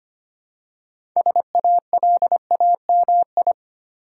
Morse code, heard as a keyed tone, about 25 wpm, 700 Hz. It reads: SALAMI
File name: practice / 25wpm / Difficult Words with Hints - 25wpm